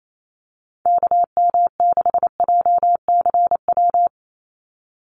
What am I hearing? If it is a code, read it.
Morse code, heard as a keyed tone, about 28 wpm, 700 Hz. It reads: KM6JCW
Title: USA Call Signs 28wpm